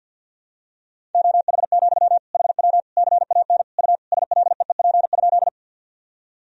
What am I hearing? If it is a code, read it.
Morse code, heard as a keyed tone, about 49 wpm, 700 Hz. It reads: OH, HW CAN U SLEEP?
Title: Sentences from Top 500 Words and Abbreviations 50wpm